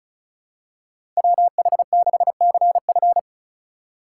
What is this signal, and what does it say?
Morse code, heard as a keyed tone, about 36 wpm, 700 Hz. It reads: WH6CF
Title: USA Call Signs - 5 Characters - 8x Character Spacing - 35wpm